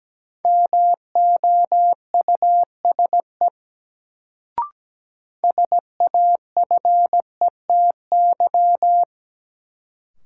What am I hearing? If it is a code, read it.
Morse code, heard as a keyed tone, about 17 wpm, 700 Hz. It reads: MOUSE SAFETY
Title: Sentences from Top 700 Words - New Word Drill - 17wpm